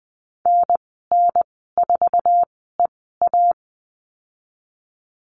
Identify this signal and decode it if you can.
Morse code, heard as a keyed tone, about 20 wpm, spaced as for 15 wpm, 700 Hz. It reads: NN4EA
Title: 076 Lesson 13g - Letter L - Call Signs 20wpm